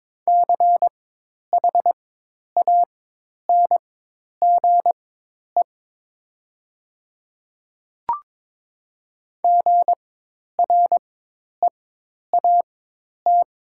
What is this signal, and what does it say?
Morse code, heard as a keyed tone, about 22 wpm, spaced as for 10 wpm, 700 Hz. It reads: CHANGE GREAT
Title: 4x Character Spacing - Top 200 Words - 22wpm